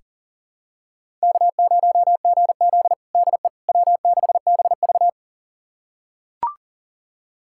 Morse code, heard as a keyed tone, about 40 wpm, 700 Hz. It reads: K0GZ DE W6BV
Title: USA Callsign DE Callsign - 4 Characters - 40wpm